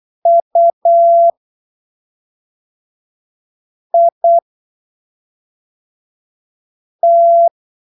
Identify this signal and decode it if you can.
Morse code, heard as a keyed tone, about 8 wpm, spaced as for 3 wpm, 680 Hz. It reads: UIT
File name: arquivos / texto / texto 12 08ppm